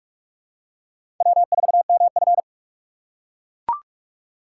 Morse code, heard as a keyed tone, about 43 wpm, 700 Hz. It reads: W4MF
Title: USA Call Signs - 4 Characters - 2x Character Spacing - 45wpm